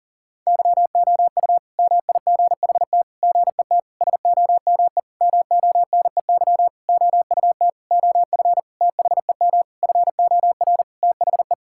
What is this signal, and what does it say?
Morse code, heard as a keyed tone, about 37 wpm, 700 Hz. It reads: YOU MIGHT GET SOME MONEY OUT OF THEM FOR THE
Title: Sentences from Top 1000 Words - Review Part 5 - 40wpm